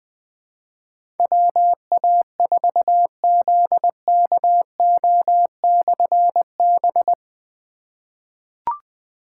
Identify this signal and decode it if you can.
Morse code, heard as a keyed tone, about 20 wpm, 700 Hz. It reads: WA4ZKO/B
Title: Difficult Call Signs 20wpm